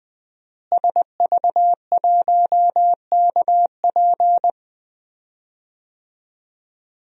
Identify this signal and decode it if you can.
Morse code, heard as a keyed tone, about 20 wpm, 700 Hz. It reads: SV1KP
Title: DXCC-100 Call Signs - 5 Characters - Repeated 2x - 20wpm